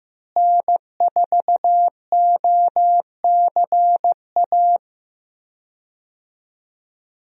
Morse code, heard as a keyed tone, about 15 wpm, 700 Hz. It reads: N4OCA
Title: USA Call Signs - 5 Characters - Repeated 3x - 15wpm